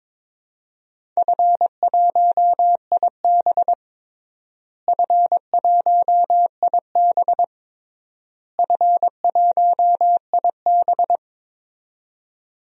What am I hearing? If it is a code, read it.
Morse code, heard as a keyed tone, about 22 wpm, 700 Hz. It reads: F1IB F1IB F1IB
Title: DXCC-100 Call Signs - 4 Characters - Repeated 3x - 22wpm